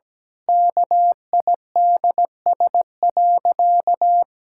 Morse code, heard as a keyed tone, about 17 wpm, 700 Hz. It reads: KIDS.
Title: Pangram Challenge v2 - 17wpm